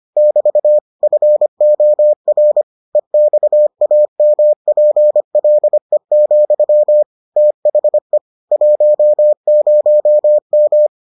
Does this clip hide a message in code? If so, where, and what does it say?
Yes: Morse code from 0.2 to 10.9 s, heard as a keyed tone, about 25 wpm, 600 Hz. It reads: <BT> FOR EXAMPLE, THE 10M